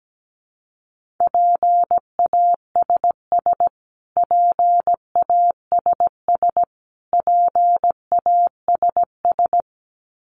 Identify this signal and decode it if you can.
Morse code, heard as a keyed tone, about 17 wpm, 700 Hz. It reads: PASS PASS PASS
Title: Top 500 Words Repeated 17wpm